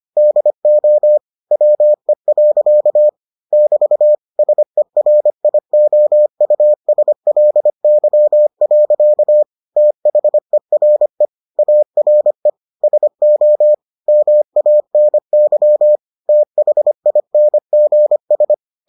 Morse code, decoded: DO WE. <BT> SERIOUSLY. THERE ARE SO MANY THINGS

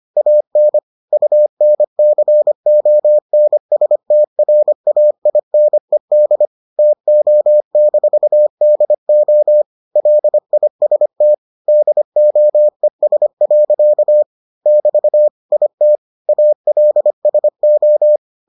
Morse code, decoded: AN UNCONSTRAINED TO-DO LIST DOES. <BT> IT ALSO

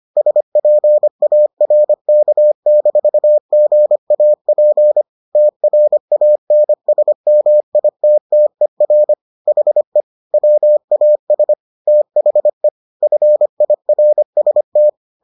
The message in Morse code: SPARK-GAP TRANSMITTER HE WAS THE FIRST

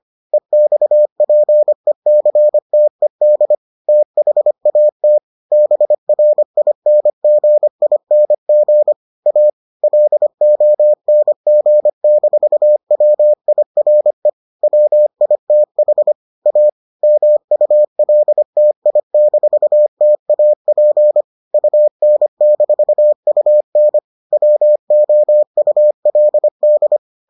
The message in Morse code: EXPECTED THAT BRINGING A LONG-WIRE WITH A MULTI-TAP UN-UN WOULD

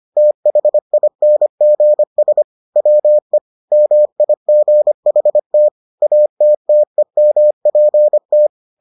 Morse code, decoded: THINGS WE MIGHT ATTEMPT

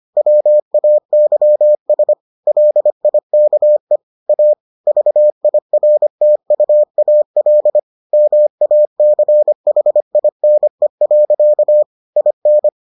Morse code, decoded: WAYS LIKE A VIRTUAL MACHINE. IN

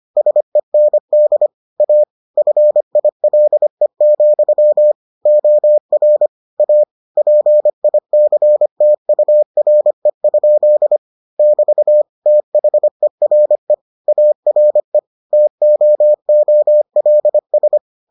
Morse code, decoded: SEND A FILE, OR A PICTURE? <BT> THERE ARE TOOLS

